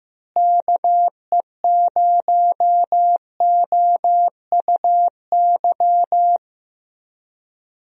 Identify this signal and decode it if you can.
Morse code, heard as a keyed tone, about 16 wpm, 700 Hz. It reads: KE0OUY
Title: USA Call Signs - 6 Characters - Repeated 3x - 15wpm